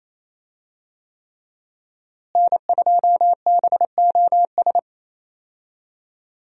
Morse code, heard as a keyed tone, about 28 wpm, 700 Hz. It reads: N2BOS
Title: USA Call Signs - 5 Characters - Repeated 3x - 28wpm